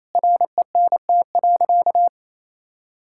Morse code, heard as a keyed tone, about 28 wpm, 700 Hz. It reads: RENT.
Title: Sentences from Top 1000 Words - Review Part 5 - 28wpm